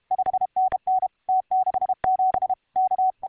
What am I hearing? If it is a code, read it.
Morse code, heard as a keyed tone, about 32 wpm, 750 Hz. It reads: 5NN T67 K